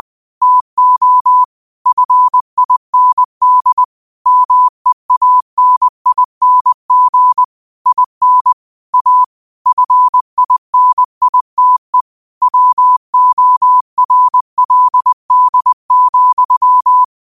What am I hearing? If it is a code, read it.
Morse code, heard as a keyed tone, about 20 wpm, 1000 Hz. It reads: TO FIND MEANING IN A FINITE WORLD,
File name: qotdfiles / 20wpm